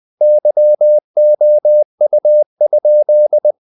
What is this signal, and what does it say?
Morse code, heard as a keyed tone, about 20 wpm, 600 Hz. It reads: YOU?
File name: the old man and the sea / the old man and the sea part 1 20wpm